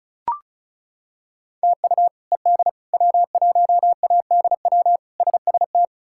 Morse code, heard as a keyed tone, about 34 wpm, 700 Hz. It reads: TU ED W1ADW SST
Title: K1USN SST QSO Practice - 35wpm